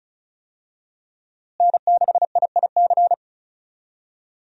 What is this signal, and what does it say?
Morse code, heard as a keyed tone, about 34 wpm, 700 Hz. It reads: N6IIC
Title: USA Call Signs - 5 Characters - Repeated 3x - 35wpm